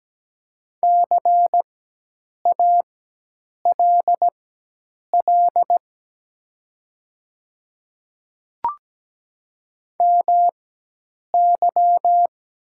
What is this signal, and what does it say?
Morse code, heard as a keyed tone, about 17 wpm, spaced as for 8 wpm, 700 Hz. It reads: CALL MY
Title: 4x Character Spacing - Top 100 Words - 17wpm